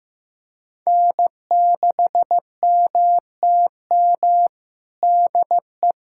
Morse code, heard as a keyed tone, about 15 wpm, 700 Hz. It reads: N6MTM DE
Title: USA Callsign DE Callsign - 5 Characters - 15wpm